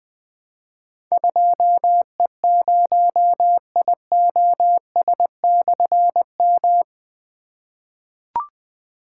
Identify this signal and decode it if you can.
Morse code, heard as a keyed tone, about 20 wpm, 700 Hz. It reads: 2E0IOS/M
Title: Difficult Call Signs 20wpm